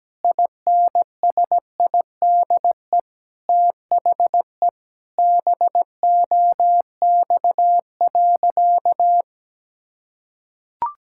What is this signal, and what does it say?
Morse code, heard as a keyed tone, about 17 wpm, 700 Hz. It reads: INSIDE THE BOX.
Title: Sentences from Top 1000 Words - Review Part 6 - 17wpm